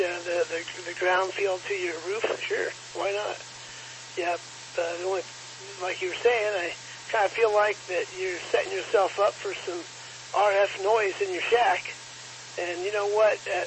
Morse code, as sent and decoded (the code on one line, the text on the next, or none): none